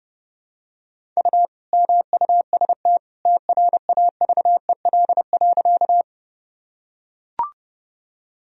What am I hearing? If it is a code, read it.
Morse code, heard as a keyed tone, about 28 wpm, 700 Hz. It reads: U MUST TRAVEL.
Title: Sentences from Top 400 Words and Abbreviations 30wpm